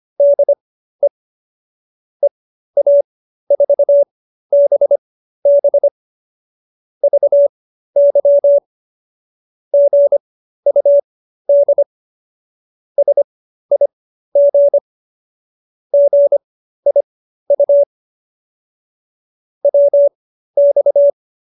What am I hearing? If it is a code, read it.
Morse code, heard as a keyed tone, about 25 wpm, spaced as for 13 wpm, 570 Hz. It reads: DE EA4BB VY GUD SIG GIU WX